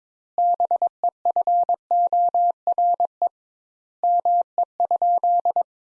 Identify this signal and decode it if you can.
Morse code, heard as a keyed tone, about 22 wpm, 700 Hz. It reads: BEFORE ME?
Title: Sentences Top 200 Words - 22wpm